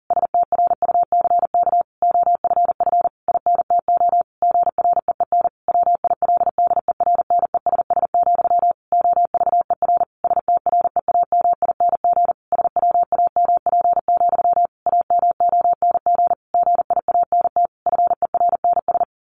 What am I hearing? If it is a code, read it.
Morse code, heard as a keyed tone, about 40 wpm, 700 Hz. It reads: STRUCK OFF INTO GREEN WILDERNESS, OVER STREAMING SWAMP, AMONG GIANT FERNS
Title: A Sound Of Thunder 40wpm